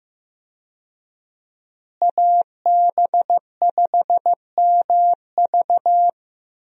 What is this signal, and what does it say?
Morse code, heard as a keyed tone, about 15 wpm, 700 Hz. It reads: AB5MV
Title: USA Call Signs - 5 Characters - Repeated 3x - 15wpm